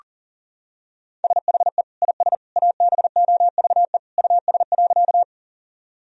Morse code, heard as a keyed tone, about 37 wpm, 700 Hz. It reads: SHE IS ABOVE US.
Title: Sentences from Top 1000 Words - Review Part 2 - 40wpm